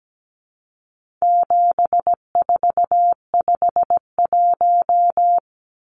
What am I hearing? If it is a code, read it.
Morse code, heard as a keyed tone, about 17 wpm, 700 Hz. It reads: 7451
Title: Numbers - 4-Digit 17wpm